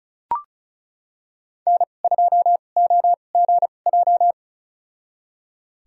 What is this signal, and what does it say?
Morse code, heard as a keyed tone, about 36 wpm, spaced as for 25 wpm, 700 Hz. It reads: N2OGJ
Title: USA Call Signs - 2x Character Spacing - 35wpm